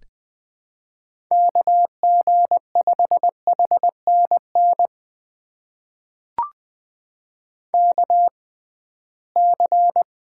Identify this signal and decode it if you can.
Morse code, heard as a keyed tone, about 20 wpm, spaced as for 15 wpm, 700 Hz. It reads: KG5HNN KC
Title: USA Call Signs - 6 Characters - 6x Character Spacing - 20wpm